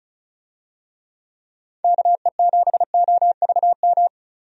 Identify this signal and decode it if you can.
Morse code, heard as a keyed tone, about 34 wpm, 700 Hz. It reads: KE7OVM
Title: USA Call Signs - 6 Characters - 35wpm